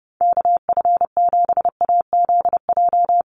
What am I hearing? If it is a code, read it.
Morse code, heard as a keyed tone, about 30 wpm, 700 Hz. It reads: KF7AZJ
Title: USA Call Signs Repeated 3x 30wpm